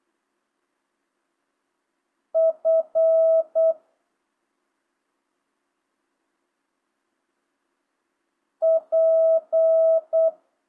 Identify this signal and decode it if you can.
Morse code, heard as a keyed tone, about 8 wpm, 660 Hz. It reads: F P